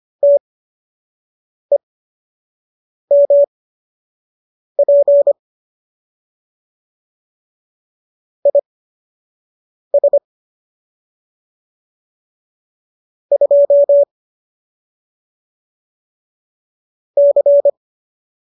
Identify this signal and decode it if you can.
Morse code, heard as a keyed tone, about 25 wpm, spaced as for 6 wpm, 570 Hz. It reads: TEMP IS 2 C
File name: qso012 06